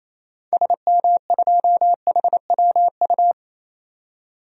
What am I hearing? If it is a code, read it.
Morse code, heard as a keyed tone, about 27 wpm, 700 Hz. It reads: SM2HWU
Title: DXCC-100 Call Signs - 6 Characters - 28wpm